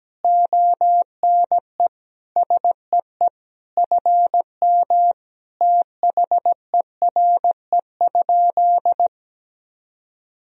Morse code, heard as a keyed tone, about 17 wpm, 700 Hz. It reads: ONE SEE FM THERE?
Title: Sentences from Top 100 Words and Abbreviations 17wpm